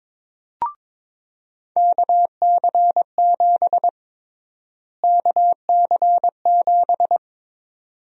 Morse code, heard as a keyed tone, about 22 wpm, 700 Hz. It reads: KC7 KC7
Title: USA Call Sign Prefixes - Repeated 2x - 22wpm